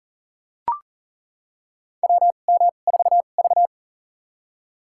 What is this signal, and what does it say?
Morse code, heard as a keyed tone, about 40 wpm, spaced as for 29 wpm, 700 Hz. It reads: WM4V
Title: USA Call Signs - 4 Characters - 2x Character Spacing - 40wpm